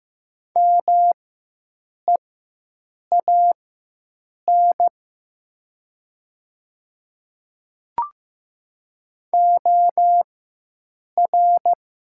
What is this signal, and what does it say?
Morse code, heard as a keyed tone, about 15 wpm, spaced as for 7 wpm, 700 Hz. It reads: MEAN OR
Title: 4x Character Spacing - Top 200 Words - 15wpm